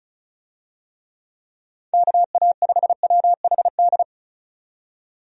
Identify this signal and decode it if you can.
Morse code, heard as a keyed tone, about 34 wpm, 700 Hz. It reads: KA5WHD
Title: USA Call Signs - 6 Characters - 35wpm